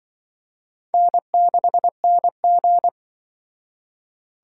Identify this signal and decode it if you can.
Morse code, heard as a keyed tone, about 24 wpm, 700 Hz. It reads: N6NG
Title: Speed Racing - USA 1x2 Call Signs 20wpm